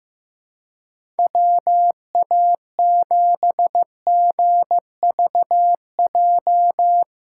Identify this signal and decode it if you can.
Morse code, heard as a keyed tone, about 15 wpm, 700 Hz. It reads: WA7GVJ
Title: USA Call Signs - 2x Character Spacing - 15wpm